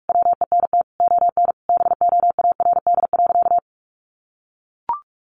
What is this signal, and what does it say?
Morse code, heard as a keyed tone, about 44 wpm, 700 Hz. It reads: WENT ON BOARD.
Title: Sentences from Top 1400 Words 45wpm